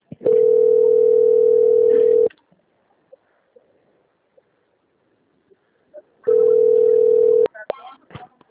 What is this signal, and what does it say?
Ringback tone (North America), 440 + 480 Hz, with speech.
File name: out- - - - - .55323